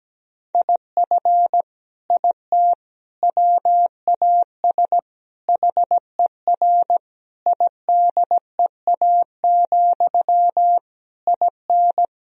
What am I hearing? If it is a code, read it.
Morse code, heard as a keyed tone, about 17 wpm, 700 Hz. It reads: IF IT WAS HER IDEA, IN
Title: Sentences from Top 1000 Words - Review Part 2 - 17wpm